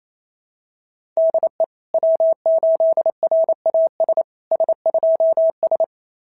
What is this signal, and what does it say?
Morse code, heard as a keyed tone, about 29 wpm, 690 Hz. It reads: DE W8RAS S2S
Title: SOTA QSO Practice 28wpm